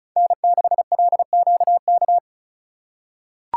Morse code, heard as a keyed tone, about 36 wpm, 700 Hz. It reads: N6LQK
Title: USA Call Signs - 5 Characters - 2x Character Spacing - 35wpm